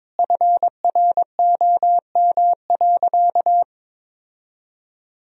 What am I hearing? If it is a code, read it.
Morse code, heard as a keyed tone, about 22 wpm, 700 Hz. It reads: FROM.
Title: Sentences from Top 1000 Words - Review Part 4 - 22wpm